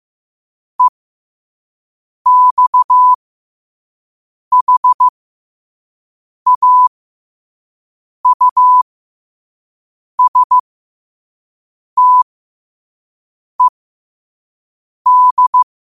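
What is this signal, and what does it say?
Morse code, heard as a keyed tone, about 15 wpm, spaced as for 5 wpm, 1000 Hz. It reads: EXHAUSTED
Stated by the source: Morse code sent at 5 wpm.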